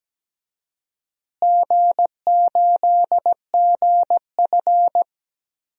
Morse code, heard as a keyed tone, about 17 wpm, 700 Hz. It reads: G8GF
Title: DXCC-100 Call Signs - 4 Characters - Repeated 2x - 17wpm